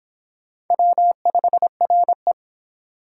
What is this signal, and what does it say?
Morse code, heard as a keyed tone, about 29 wpm, 700 Hz. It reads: W5RE